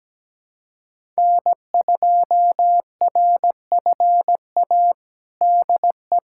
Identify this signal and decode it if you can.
Morse code, heard as a keyed tone, about 17 wpm, 700 Hz. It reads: N2RFA DE
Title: USA Callsign DE Callsign - 5 Characters - 17wpm